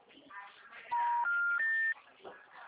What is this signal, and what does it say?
Special information tone, with speech.